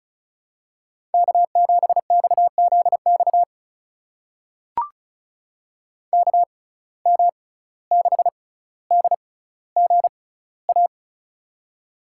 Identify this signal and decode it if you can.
Morse code, heard as a keyed tone, about 36 wpm, spaced as for 17 wpm, 700 Hz. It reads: K7XZX KM6DGA
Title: USA Call Signs - 6x Character Spacing - 35wpm